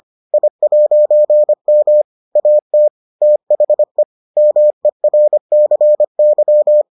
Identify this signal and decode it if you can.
Morse code, heard as a keyed tone, about 25 wpm, 600 Hz. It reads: I'M AT THE MERCY